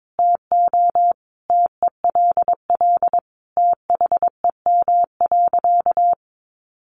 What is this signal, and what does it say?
Morse code, heard as a keyed tone, about 22 wpm, 700 Hz. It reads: TO TELL THEM.
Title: Sentences from Top 700 Words 22wpm